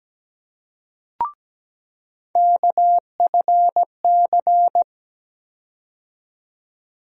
Morse code, heard as a keyed tone, about 17 wpm, 700 Hz. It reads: KFC
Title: USA Call Sign Postfixes - 3 Letters 17wpm